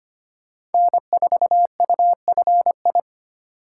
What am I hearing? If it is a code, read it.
Morse code, heard as a keyed tone, about 32 wpm, 700 Hz. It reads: N4UFI